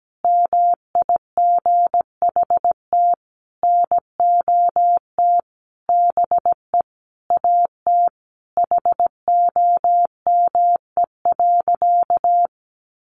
Morse code, heard as a keyed tone, about 17 wpm, 700 Hz. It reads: MIGHT NOT BE AT HOME.
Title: Sentences from Top 1500 Words 17wpm